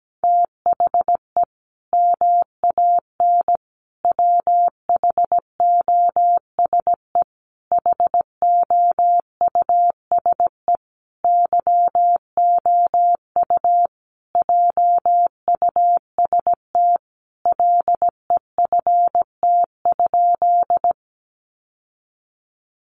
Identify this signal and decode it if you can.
Morse code, heard as a keyed tone, about 17 wpm, 700 Hz. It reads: THE MAN WHOSE HOUSE YOU JUST LEFT?
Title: Sentences from Top 1200 Words 17wpm